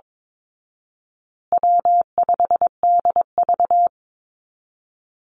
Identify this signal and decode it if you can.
Morse code, heard as a keyed tone, about 22 wpm, 700 Hz. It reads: W5DV
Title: Easy Call Signs Repeated 3x - 22wpm